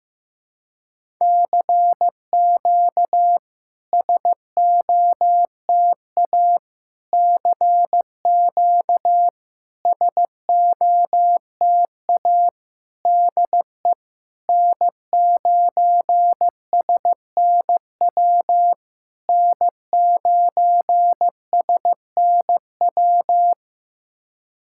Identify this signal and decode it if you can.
Morse code, heard as a keyed tone, about 16 wpm, 700 Hz. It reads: CQ SOTA CQ SOTA DE N9SNW N9SNW
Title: SOTA QSO Practice 15wpm